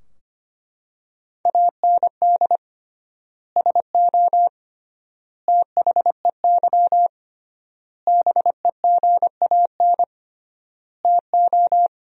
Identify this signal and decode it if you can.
Morse code, heard as a keyed tone, about 25 wpm, 700 Hz. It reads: AND SO THEY BEGAN TO
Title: 3x Word-Spacing - Sentences from Top 400 Words - 25wpm